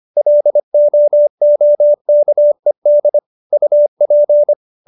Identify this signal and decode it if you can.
Morse code, heard as a keyed tone, about 25 wpm, 590 Hz. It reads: LOOKED UP